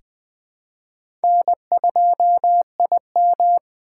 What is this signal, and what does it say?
Morse code, heard as a keyed tone, about 20 wpm, 700 Hz. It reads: N2IM
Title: USA Call Signs - 3x Character Spacing - 20wpm